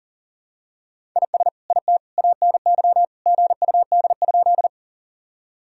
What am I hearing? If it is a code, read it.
Morse code, heard as a keyed tone, about 37 wpm, 700 Hz. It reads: IS IT ANY GUD?
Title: Sentences from Top 200 Words and Abbreviations 40wpm